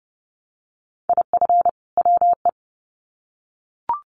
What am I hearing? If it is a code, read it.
Morse code, heard as a keyed tone, about 30 wpm, 700 Hz. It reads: IF WE
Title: Sets of 2 Words 30wpm